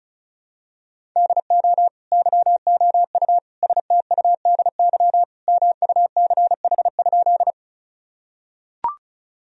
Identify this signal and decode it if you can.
Morse code, heard as a keyed tone, about 36 wpm, 690 Hz. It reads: DO YOU STUDY MUCH?
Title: Sentences from Top 1000 Words - Review Part 1 - 35wpm